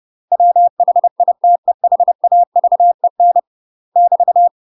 Morse code, heard as a keyed tone, about 30 wpm, 700 Hz. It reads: WHITEHAVEN <BT>
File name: UK MASTER QSOs TRAINING / 30WPM / UK MASTER QSOs 0026